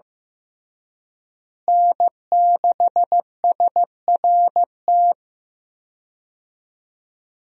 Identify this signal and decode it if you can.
Morse code, heard as a keyed tone, about 15 wpm, 700 Hz. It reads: N6SRT
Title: USA Call Signs - 5 Characters - Repeated 2x - 15wpm